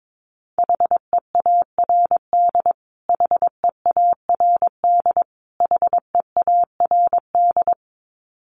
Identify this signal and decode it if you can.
Morse code, heard as a keyed tone, about 22 wpm, 700 Hz. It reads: HEARD HEARD HEARD
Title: Top 500 Words Repeated 22wpm